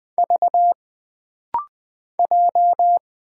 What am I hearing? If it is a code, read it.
Morse code, heard as a keyed tone, about 20 wpm, 700 Hz. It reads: V J
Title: Single Letter-Number Alt - Rapid-Fire 20wpm